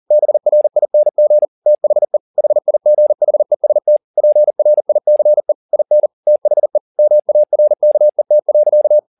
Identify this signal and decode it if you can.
Morse code, heard as a keyed tone, about 40 wpm, 600 Hz. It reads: BRING THE HIGHEST PRICE IN THE MARKET.
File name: the old man and the sea 2 40wpm